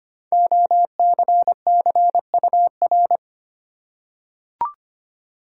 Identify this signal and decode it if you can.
Morse code, heard as a keyed tone, about 25 wpm, 700 Hz. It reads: OCCUR
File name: 5 Letter Words - First 3 Letters Given - 25wpm